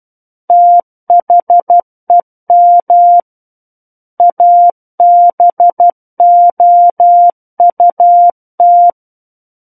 Morse code, heard as a keyed tone, about 12 wpm, 700 Hz. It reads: THEM ABOUT
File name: Book One / 03-Chapter 3 - 12 WPM